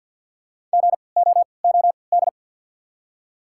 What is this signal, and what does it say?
Morse code, heard as a keyed tone, about 53 wpm, spaced as for 29 wpm, 690 Hz. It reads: GOOD
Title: 3x Character Spacing - Top 200 Words - 50wpm